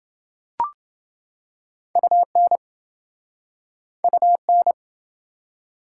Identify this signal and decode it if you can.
Morse code, heard as a keyed tone, about 28 wpm, 700 Hz. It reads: UN UN